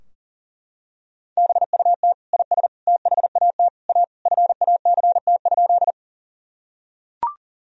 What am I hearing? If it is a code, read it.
Morse code, heard as a keyed tone, about 40 wpm, 700 Hz. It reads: BUT IS THAT A FACT?
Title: Sentences from Top 1000 Words - Review Part 3 - 40wpm